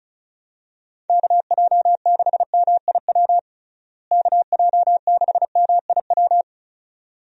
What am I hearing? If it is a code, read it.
Morse code, heard as a keyed tone, about 34 wpm, 700 Hz. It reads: KJ6MIW KJ6MIW
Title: USA Call Signs - 6 Characters - Repeated 2x - 35wpm